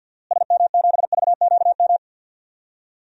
Morse code, heard as a keyed tone, about 46 wpm, 700 Hz. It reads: SM7VQM
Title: DXCC-100 Call Signs - 6 Characters - Repeated 2x - 50wpm